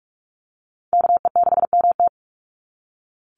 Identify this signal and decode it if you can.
Morse code, heard as a keyed tone, about 46 wpm, 700 Hz. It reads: KE6MT
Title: NA SOTA Chaser Callsigns 45wpm